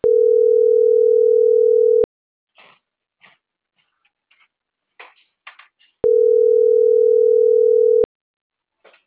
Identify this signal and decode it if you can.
Ringback tone (North America), 440 + 480 Hz.